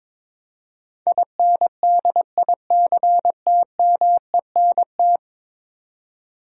Morse code, heard as a keyed tone, about 22 wpm, 700 Hz. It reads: INDICTMENT
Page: morse-code-ninja-practice-22wpm